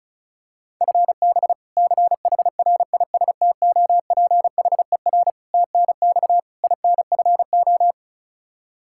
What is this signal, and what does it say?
Morse code, heard as a keyed tone, about 36 wpm, 700 Hz. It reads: FB CHRISTOPHER TNX INFO
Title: K1USN SST QSO Practice 35wpm